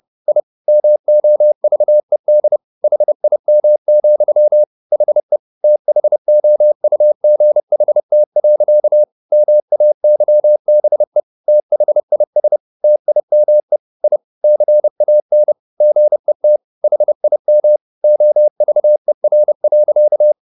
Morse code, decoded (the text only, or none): I MOVED HIM, HE THOUGHT. MAYBE THIS TIME I CAN GET HIM OVER.